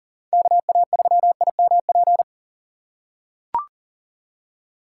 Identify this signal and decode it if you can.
Morse code, heard as a keyed tone, about 37 wpm, 700 Hz. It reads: KA3IMP
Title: USA Call Signs - 6 Characters - 4x Character Spacing - 40wpm